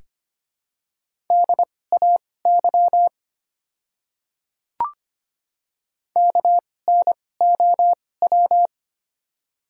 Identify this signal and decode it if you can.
Morse code, heard as a keyed tone, about 25 wpm, spaced as for 18 wpm, 700 Hz. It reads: DAY KNOW
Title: 2x Character Spacing - Top 100 Words - 25wpm